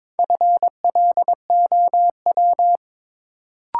Morse code, heard as a keyed tone, about 22 wpm, 700 Hz. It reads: FLOW
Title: Sentences from Top 1200 Words - New Word Drill Alt - 22wpm